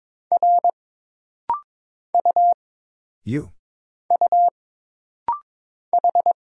Morse code, heard as a keyed tone, about 22 wpm, 700 Hz. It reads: R U U H
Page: This practice set is composed of Letters, Numbers, common punctuations, and common prosigns.